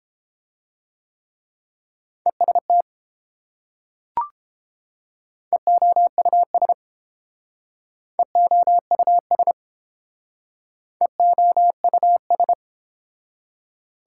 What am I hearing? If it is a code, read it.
Morse code, heard as a keyed tone, about 29 wpm, 700 Hz. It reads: EST EOUS EOUS EOUS